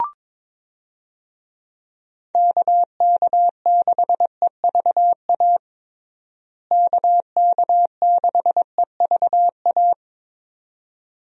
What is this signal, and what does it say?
Morse code, heard as a keyed tone, about 22 wpm, 700 Hz. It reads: KK6EVA KK6EVA
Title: USA Call Signs - 6 Characters - Repeated 2x - 22wpm